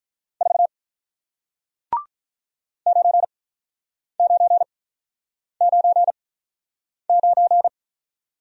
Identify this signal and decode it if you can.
Morse code, heard as a keyed tone, about 48 wpm, spaced as for 9 wpm, 700 Hz. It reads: 4 9999